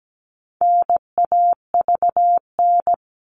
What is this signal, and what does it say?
Morse code, heard as a keyed tone, about 17 wpm, 700 Hz. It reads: NAVN
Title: All Cut Numbers - 4-Digit 17wpm